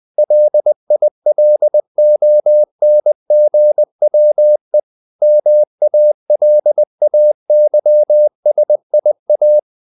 Morse code, heard as a keyed tone, about 20 wpm, 600 Hz. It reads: LILONGWE MALAYSIA
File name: CW / Practice / NationalCapitals20